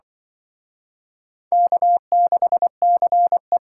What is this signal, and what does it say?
Morse code, heard as a keyed tone, about 25 wpm, 700 Hz. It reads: K6CE
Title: Speed Racing - USA 1x2 Call Signs 20wpm